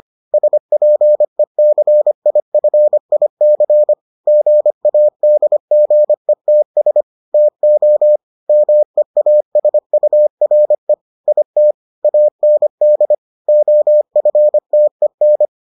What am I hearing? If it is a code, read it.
Morse code, heard as a keyed tone, about 25 wpm, 600 Hz. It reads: SPECIFIC GADGETS TO MEASURE IT AND OFTEN